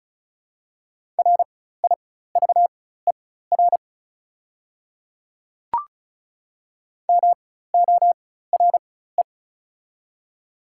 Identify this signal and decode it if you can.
Morse code, heard as a keyed tone, about 36 wpm, spaced as for 16 wpm, 690 Hz. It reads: RIVER MORE
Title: 4x Character Spacing - Top 300 Words - 35wpm